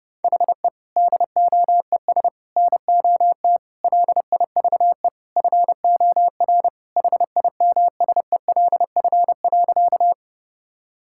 Morse code, heard as a keyed tone, about 30 wpm, 700 Hz. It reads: HE DOES NOT LIVE FOR HIMSELF.
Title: Sentences from Top 1600 Words 30wpm